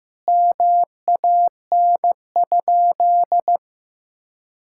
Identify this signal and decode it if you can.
Morse code, heard as a keyed tone, about 15 wpm, 700 Hz. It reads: MAN?
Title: Sentences from Top 1000 Words - Review Part 5 - 15wpm